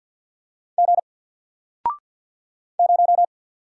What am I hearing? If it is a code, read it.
Morse code, heard as a keyed tone, about 53 wpm, 700 Hz. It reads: G 0
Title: Single Letter-Number 50wpm